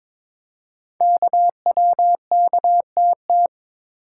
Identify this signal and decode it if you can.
Morse code, heard as a keyed tone, about 22 wpm, 700 Hz. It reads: KWKTT
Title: Code Groups - 5 Letters - 22wpm